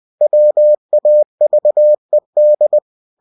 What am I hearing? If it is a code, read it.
Morse code, heard as a keyed tone, about 20 wpm, 600 Hz. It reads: WAVED
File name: cw-books / winnie-the-pooh / winnie-20wpm-page-0106